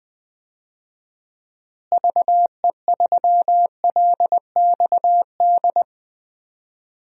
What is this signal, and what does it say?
Morse code, heard as a keyed tone, about 20 wpm, 700 Hz. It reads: VE3LXD
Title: DXCC-100 Call Signs - 6 Characters - Repeated 3x - 20wpm